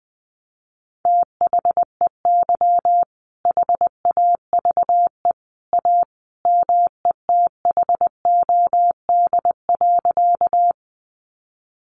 Morse code, heard as a keyed tone, about 20 wpm, 700 Hz. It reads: THEY HAVE A METHOD.
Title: Sentences from Top 600 Words 20wpm